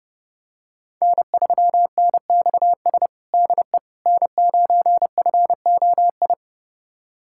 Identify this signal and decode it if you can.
Morse code, heard as a keyed tone, about 32 wpm, 700 Hz. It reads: N3NXS DE N9FOI
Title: USA Callsign DE Callsign - 5 Characters - 30wpm